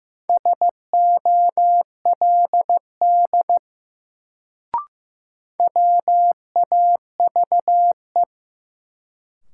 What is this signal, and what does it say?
Morse code, heard as a keyed tone, about 15 wpm, 700 Hz. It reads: SOLD WAVE
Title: Sentences from Top 1400 Words - New Word Drill Alt - 15wpm